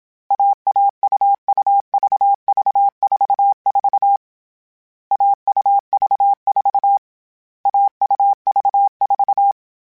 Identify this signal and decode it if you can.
Morse code, heard as a keyed tone, about 27 wpm, 800 Hz. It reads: AAUUVV44 AUV4 AUV4
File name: CW Code Groups-mp3s / Sets mp3s / AUV AUV4 sp @25